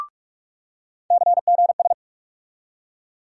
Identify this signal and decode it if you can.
Morse code, heard as a keyed tone, about 46 wpm, 700 Hz. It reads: CGS
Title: USA Call Sign Postfixes - 3 Letters 45wpm